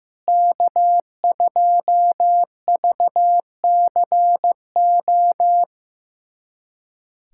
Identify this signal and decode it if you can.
Morse code, heard as a keyed tone, about 15 wpm, 700 Hz. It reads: K2VCO
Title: USA Callsign DE Callsign - 5 Characters - 15wpm